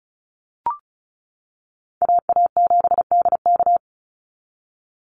Morse code, heard as a keyed tone, about 34 wpm, 700 Hz. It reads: AA7DK
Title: NA SOTA Chaser Callsigns 35wpm